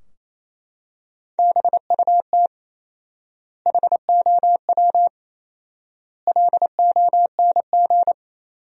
Morse code, heard as a keyed tone, about 27 wpm, 700 Hz. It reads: BUT HOW LONG
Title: 4x Word-Spacing - Sentences from Top 500 Words - 28wpm